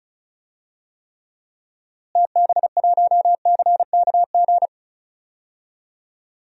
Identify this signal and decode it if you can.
Morse code, heard as a keyed tone, about 36 wpm, 700 Hz. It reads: TB1CKG
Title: DXCC-100 Call Signs - 6 Characters - Repeated 3x - 35wpm